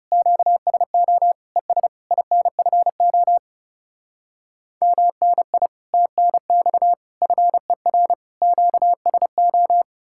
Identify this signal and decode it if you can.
Morse code, heard as a keyed tone, about 40 wpm, 700 Hz. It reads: QSO ES INFO MNI TNX FER QSO